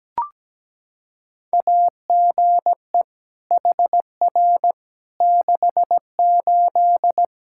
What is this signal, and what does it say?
Morse code, heard as a keyed tone, about 17 wpm, 700 Hz. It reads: AGE HR 68
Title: Practice QSO - Exchange 3 - More Info - 17wpm